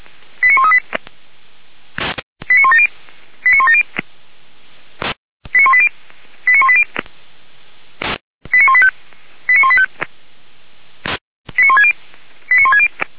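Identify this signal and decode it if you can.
Five-tone selective call (ZVEI), heard as quick runs of five tones, tones of about 70 ms. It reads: 89138 89170 89170 89180 89180 89185 89185 89160 89160
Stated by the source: ZVEI five-tone selective call 89136 received by SDR, from Feuerwehr Orlamünde audio server.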